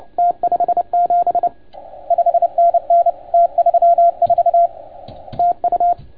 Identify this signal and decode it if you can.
Morse code, heard as a keyed tone, about 29 wpm, 690 Hz. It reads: T57 5NN T34 TU
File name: Archiv / CUC287 / OK1FGD CUC 287